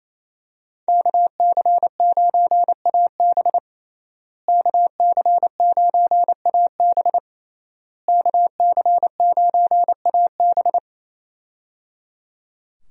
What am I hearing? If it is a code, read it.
Morse code, heard as a keyed tone, about 28 wpm, 700 Hz. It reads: KC9AB KC9AB KC9AB
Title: USA Call Signs - 5 Characters - Repeated 3x - 28wpm